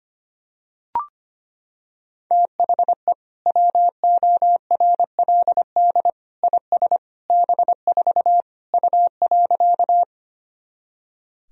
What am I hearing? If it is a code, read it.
Morse code, heard as a keyed tone, about 25 wpm, 700 Hz. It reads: THE WORLD IS B4 U.
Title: Sentences from Top 200 Words and Abbreviations 25wpm